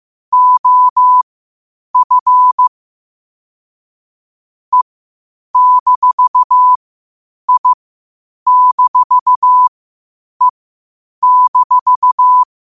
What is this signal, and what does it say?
Morse code, heard as a keyed tone, about 15 wpm, spaced as for 8 wpm, 1000 Hz. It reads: OF E-I-E-
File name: qotdfiles / 8wpm